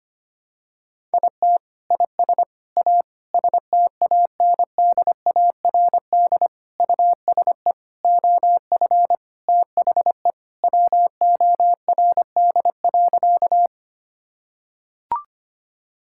Morse code, heard as a keyed tone, about 25 wpm, 700 Hz. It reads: IT IS A STANDARD USE OF THE WORD.
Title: Sentences from Top 1000 Words - Review Part 6 - 25wpm